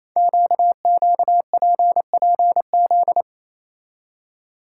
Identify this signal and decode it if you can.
Morse code, heard as a keyed tone, about 27 wpm, 700 Hz. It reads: QQPPZ
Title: Code Groups - 5 Letters - 28wpm